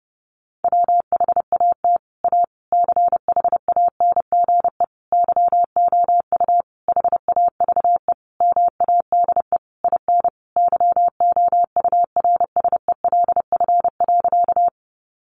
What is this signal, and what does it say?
Morse code, heard as a keyed tone, about 30 wpm, 700 Hz. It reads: WHAT A CHANGE YOU HAVE MADE IN YOURSELF.
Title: Sentences from Top 600 Words 30wpm